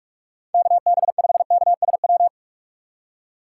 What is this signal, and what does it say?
Morse code, heard as a keyed tone, about 46 wpm, 700 Hz. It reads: KB5KSW
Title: USA Call Signs - 6 Characters - 3x Character Spacing - 45wpm